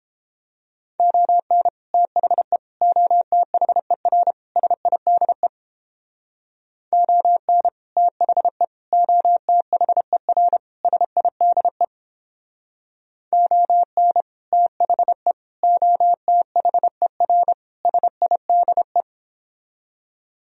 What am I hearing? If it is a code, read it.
Morse code, heard as a keyed tone, about 29 wpm, 700 Hz. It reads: ON THE OTHER SIDE ON THE OTHER SIDE ON THE OTHER SIDE